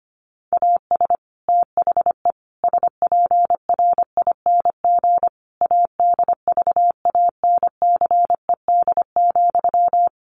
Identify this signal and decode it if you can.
Morse code, heard as a keyed tone, about 25 wpm, 700 Hz. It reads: AS THE SPRING ADVANCED,
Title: Sentences from Top 1200 Words 25wpm